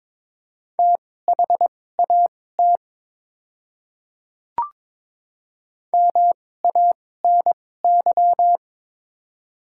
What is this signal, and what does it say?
Morse code, heard as a keyed tone, about 22 wpm, spaced as for 16 wpm, 700 Hz. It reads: THAT MANY
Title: 2x Character Spacing - Top 300 Words - 22wpm